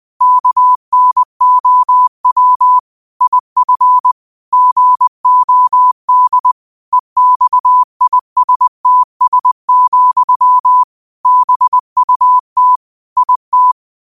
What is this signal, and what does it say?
Morse code, heard as a keyed tone, about 20 wpm, 1000 Hz. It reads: KNOW IF GOD EXISTS, BUT IT
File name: qotdfiles / 20wpm